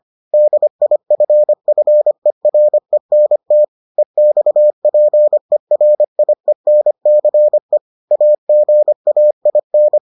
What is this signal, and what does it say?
Morse code, heard as a keyed tone, about 25 wpm, 600 Hz. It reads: DIFFERENT EXPERIENCE AGAIN